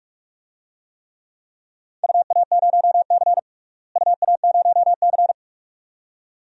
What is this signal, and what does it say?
Morse code, heard as a keyed tone, about 41 wpm, 700 Hz. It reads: UA0C UA0C
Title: DXCC-100 Call Signs - 4 Characters - Repeated 2x - 45wpm